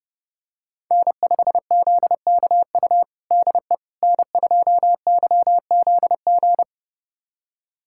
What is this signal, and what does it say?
Morse code, heard as a keyed tone, about 30 wpm, 700 Hz. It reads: N5ZKU DE N2YZG
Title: USA Callsign DE Callsign - 5 Characters - 30wpm